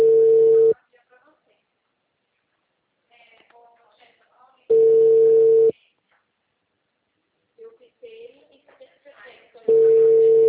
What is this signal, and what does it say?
Ringback tone (China, 450 Hz), with speech.
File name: out- -2005- - - .80752